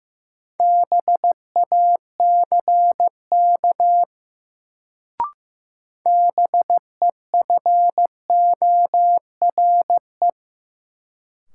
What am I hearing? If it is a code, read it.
Morse code, heard as a keyed tone, about 15 wpm, 700 Hz. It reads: BACK BEFORE
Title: Sentences from Top 200 Words - New Word Drill - 15wpm